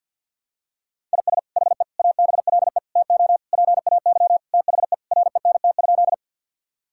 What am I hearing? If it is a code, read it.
Morse code, heard as a keyed tone, about 50 wpm, 700 Hz. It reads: IS HE ABLE TO PAY THE RENT?
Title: Sentences from Top 1000 Words - Review Part 5 - 50wpm